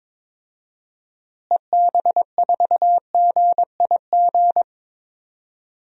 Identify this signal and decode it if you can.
Morse code, heard as a keyed tone, about 22 wpm, 700 Hz. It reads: EB4GIG
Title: DXCC-100 Call Signs - 6 Characters - Repeated 2x - 22wpm